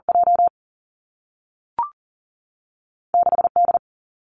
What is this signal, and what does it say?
Morse code, heard as a keyed tone, about 39 wpm, 700 Hz. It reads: J 6 D
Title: Two Challenging Letter-Number Combinations 40wpm